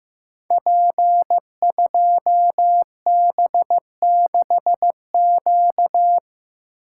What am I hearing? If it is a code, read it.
Morse code, heard as a keyed tone, about 15 wpm, 700 Hz. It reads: P2B6Q
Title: Code Groups - 5 Letter:Numbers - 15wpm